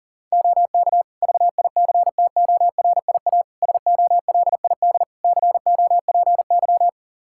Morse code, heard as a keyed tone, about 37 wpm, 700 Hz. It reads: OK VICTORIA SOLID COPY
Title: Practice QSO - Exchange 3 - More Info - 40wpm